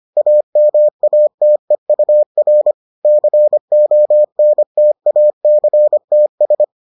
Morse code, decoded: AMATEUR CONTACTS

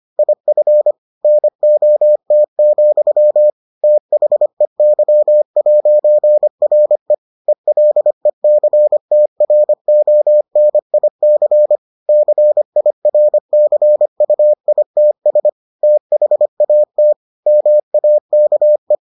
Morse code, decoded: IF NOT, THEY'RE ELECTRONIC CIRCUITS THAT MAKE

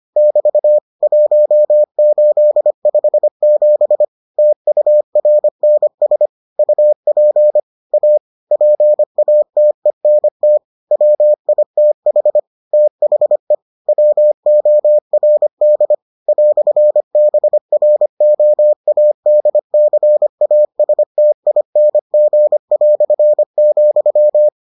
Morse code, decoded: <BT> 1857 TURNS UP A PATENT WITH THE WORD "BROADCASTING",

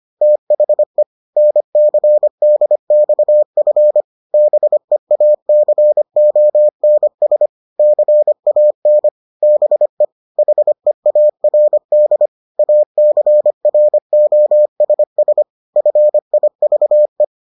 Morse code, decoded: THE NCDXF BEACONS CAN BE HEARD ACROSS FIVE

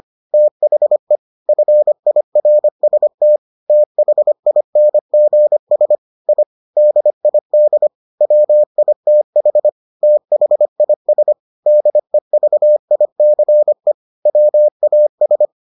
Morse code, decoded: THE FIRST THINGS I DID WITH THIS DEVICE WAS

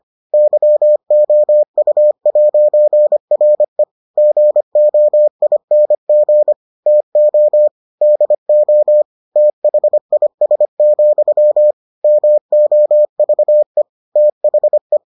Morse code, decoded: YOU'RE GOING TO DO THIS, MOVE THE

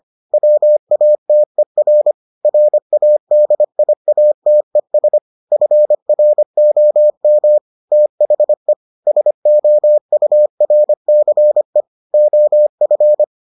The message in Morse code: WATER RADIATES FROM THE SOURCE OF

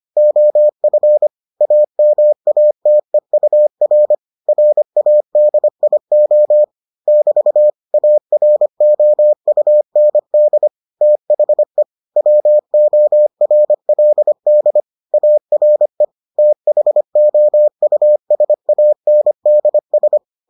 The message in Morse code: OF AMATEUR RADIO <BT> AROUND THE WORLD ARE THOUSANDS

